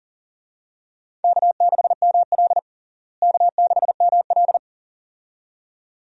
Morse code, heard as a keyed tone, about 40 wpm, 700 Hz. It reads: K6ML K6ML
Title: USA Call Signs - 4 Characters - Repeated 2x - 40wpm